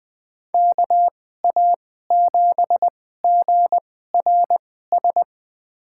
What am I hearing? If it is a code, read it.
Morse code, heard as a keyed tone, about 20 wpm, spaced as for 14 wpm, 700 Hz. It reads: KA7GRS
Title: USA Call Signs - 6 Characters - 2x Character Spacing - 20wpm